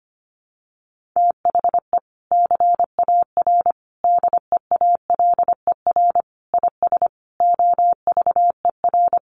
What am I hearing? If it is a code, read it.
Morse code, heard as a keyed tone, about 25 wpm, 700 Hz. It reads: THE CAR DEALER IS OVER
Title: Sentences from Top 1100 Words 25wpm